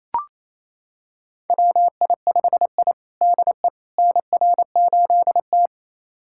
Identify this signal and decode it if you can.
Morse code, heard as a keyed tone, about 28 wpm, 700 Hz. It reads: WI5I DE NR8T
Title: USA Callsign DE Callsign - 4 Characters - 28wpm